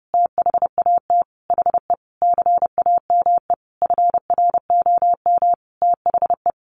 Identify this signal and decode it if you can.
Morse code, heard as a keyed tone, about 30 wpm, 700 Hz. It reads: THAT HE CAME FROM THE
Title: Sentences from Top 1500 Words 30wpm